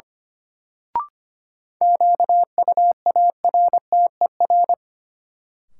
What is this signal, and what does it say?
Morse code, heard as a keyed tone, about 25 wpm, 700 Hz. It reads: QUARTER
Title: Sentences from Top 1300 Words - New Word Drill Alt - 25wpm